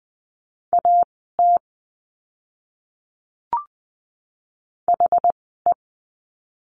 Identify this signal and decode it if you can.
Morse code, heard as a keyed tone, about 20 wpm, spaced as for 14 wpm, 700 Hz. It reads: AT HE
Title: 192 Lesson 31f - Letter v - Words 20wpm